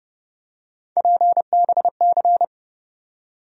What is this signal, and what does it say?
Morse code, heard as a keyed tone, about 30 wpm, 700 Hz. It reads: PBC
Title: USA Call Sign Postfixes - 3 Letters 30wpm